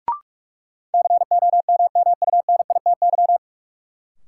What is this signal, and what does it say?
Morse code, heard as a keyed tone, about 42 wpm, 700 Hz. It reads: COMMUNITY
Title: Sentences from Top 1200 Words - New Word Drill Alt - 45wpm